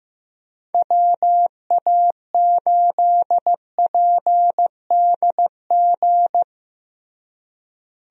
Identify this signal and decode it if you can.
Morse code, heard as a keyed tone, about 15 wpm, 700 Hz. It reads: WA8PDG
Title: USA Call Signs - 6 Characters - Repeated 3x - 15wpm